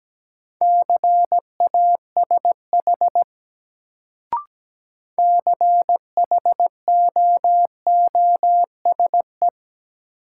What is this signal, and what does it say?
Morse code, heard as a keyed tone, about 17 wpm, 700 Hz. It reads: CASH CHOOSE
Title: Sentences from Top 700 Words - New Word Drill - 17wpm